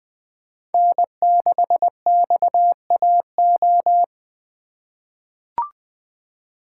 Morse code, heard as a keyed tone, about 20 wpm, 700 Hz. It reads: N6XAO
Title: USA Call Signs - 5 Characters - 2x Character Spacing - 20wpm